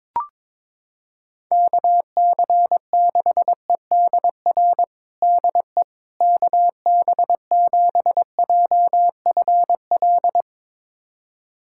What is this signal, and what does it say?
Morse code, heard as a keyed tone, about 22 wpm, 700 Hz. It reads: KC6EDR DE KB7JFL
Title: USA Callsign DE Callsign 22wpm